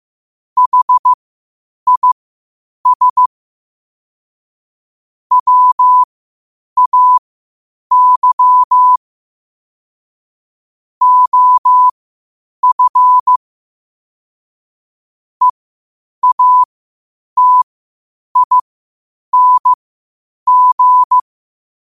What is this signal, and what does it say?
Morse code, heard as a keyed tone, about 15 wpm, spaced as for 8 wpm, 1000 Hz. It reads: HIS WAY OF EATING
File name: qotdfiles / 8wpm